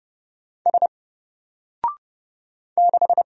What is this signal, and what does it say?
Morse code, heard as a keyed tone, about 30 wpm, 700 Hz. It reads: S 6
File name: Single-Letter-Number-30wpm